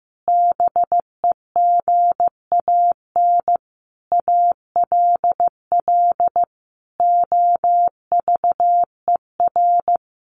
Morse code, heard as a keyed tone, about 14 wpm, 700 Hz. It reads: BEGAN ALL OVER
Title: Sentences from Top 500 Words 15wpm